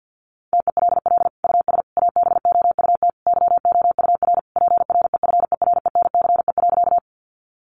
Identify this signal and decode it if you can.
Morse code, heard as a keyed tone, about 50 wpm, 700 Hz. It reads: TELL US ABOUT YOUR PREFERENCE.
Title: Sentences from Top 1500 Words 50wpm